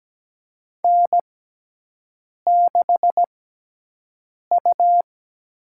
Morse code, heard as a keyed tone, about 17 wpm, spaced as for 6 wpm, 700 Hz. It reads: N6U
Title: USA Call Signs - 6x Character Spacing - 17wpm